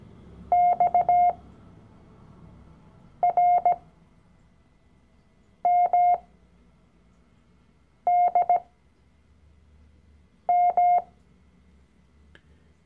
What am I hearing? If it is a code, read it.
Morse code, heard as a keyed tone, about 17 wpm, spaced as for 4 wpm, 710 Hz. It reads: XRMDM
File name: CW-Online-Kurs / Anfänger / Lektion 012